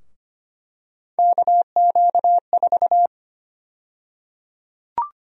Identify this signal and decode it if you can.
Morse code, heard as a keyed tone, about 25 wpm, 700 Hz. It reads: KQ4
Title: USA Call Sign Prefixes - 25wpm